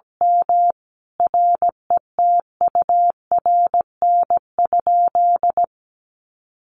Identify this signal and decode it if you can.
Morse code, heard as a keyed tone, about 17 wpm, 700 Hz. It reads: M RETURN?
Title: Sentences from Top 1300 Words 17wpm